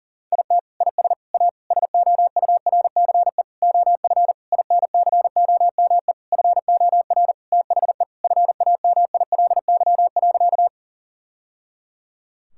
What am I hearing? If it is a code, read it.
Morse code, heard as a keyed tone, about 40 wpm, 700 Hz. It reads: IT IS A SOURCE OF INCOME FOR THE FAMILY.
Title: Sentences from Top 1000 Words - Review Part 5 - 40wpm